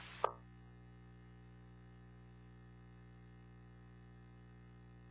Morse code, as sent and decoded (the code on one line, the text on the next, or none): none